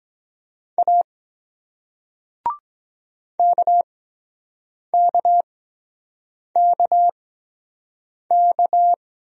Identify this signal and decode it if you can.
Morse code, heard as a keyed tone, about 23 wpm, spaced as for 7 wpm, 700 Hz. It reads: A KKKK